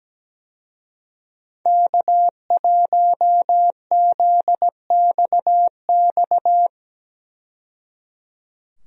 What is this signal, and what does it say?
Morse code, heard as a keyed tone, about 17 wpm, 700 Hz. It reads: K1ZXX
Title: USA Call Signs - 5 Characters - 17wpm